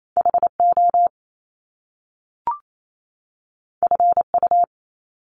Morse code, heard as a keyed tone, about 28 wpm, 700 Hz. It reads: HO FU
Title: Top Two Letter Combinations in Words 28wpm